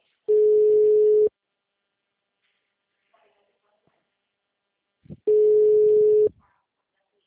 Ringback tone (Europe, 425 Hz).